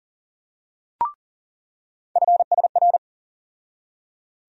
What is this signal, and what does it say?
Morse code, heard as a keyed tone, about 40 wpm, 700 Hz. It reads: FSR
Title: USA Call Sign Postfixes - 3 Letters 40wpm